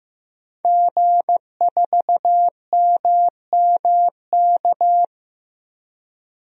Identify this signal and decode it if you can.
Morse code, heard as a keyed tone, about 15 wpm, 700 Hz. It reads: G4MMK
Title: DXCC-100 Call Signs - 5 Characters - Repeated 3x - 15wpm